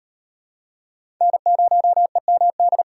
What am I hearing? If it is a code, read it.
Morse code, heard as a keyed tone, about 38 wpm, 700 Hz. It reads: N0EMD